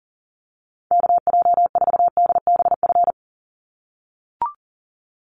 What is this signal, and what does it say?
Morse code, heard as a keyed tone, about 40 wpm, 700 Hz. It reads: KJ4DBF
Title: USA Call Signs 40wpm